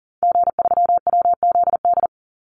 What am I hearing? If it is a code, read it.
Morse code, heard as a keyed tone, about 40 wpm, 700 Hz. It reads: G3WZD
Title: Call Signs Repeated 2x - 40wpm